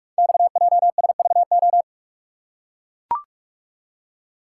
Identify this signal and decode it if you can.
Morse code, heard as a keyed tone, about 44 wpm, 690 Hz. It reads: XJSVO
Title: Code Groups - 5 Letters - 45wpm